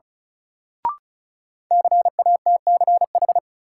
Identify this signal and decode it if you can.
Morse code, heard as a keyed tone, about 36 wpm, 700 Hz. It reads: CATCH
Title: Sentences from Top 1500 Words - New Word Drill - 35wpm